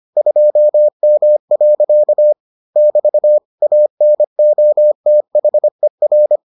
Morse code, heard as a keyed tone, about 25 wpm, 600 Hz. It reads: 2M. <BT> ANOTHER